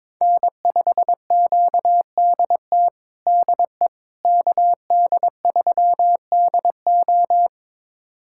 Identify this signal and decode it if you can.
Morse code, heard as a keyed tone, about 22 wpm, 700 Hz. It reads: N5QDT DE KD3DO
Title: USA Callsign DE Callsign - 5 Characters - 22wpm